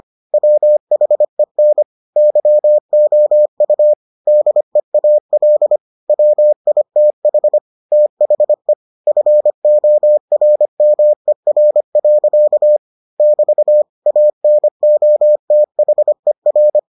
Morse code, heard as a keyed tone, about 25 wpm, 600 Hz. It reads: WHEN YOU DEAL WITH THE FORMER. <BT> ANOTHER